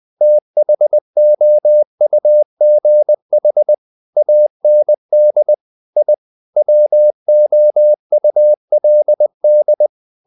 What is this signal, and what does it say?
Morse code, heard as a keyed tone, about 20 wpm, 600 Hz. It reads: THOUGH AND I WOULD
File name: the old man and the sea part 2 20wpm